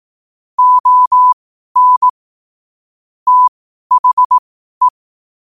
Morse code, heard as a keyed tone, about 18 wpm, spaced as for 12 wpm, 1000 Hz. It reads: ON THE
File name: qotdfiles / 12wpm